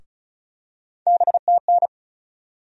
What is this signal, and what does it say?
Morse code, heard as a keyed tone, about 36 wpm, 700 Hz. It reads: BTN
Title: USA Call Sign Postfixes - 3 Letters 35wpm